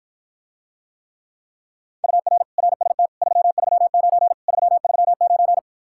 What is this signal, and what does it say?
Morse code, heard as a keyed tone, about 53 wpm, 700 Hz. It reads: UR RST 339 339